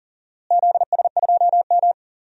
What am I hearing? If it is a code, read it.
Morse code, heard as a keyed tone, about 40 wpm, 700 Hz. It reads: ZS2M
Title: DXCC-100 Call Signs - 4 Characters - Repeated 3x - 40wpm